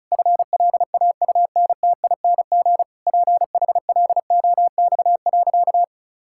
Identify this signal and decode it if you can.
Morse code, heard as a keyed tone, about 36 wpm, 700 Hz. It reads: FLAUNTING PHLOX.
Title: Pangram Challenge v2 - 35wpm